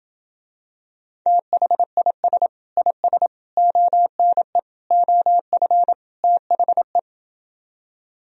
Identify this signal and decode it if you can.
Morse code, heard as a keyed tone, about 28 wpm, 700 Hz. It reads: THIS IS ONE OF THE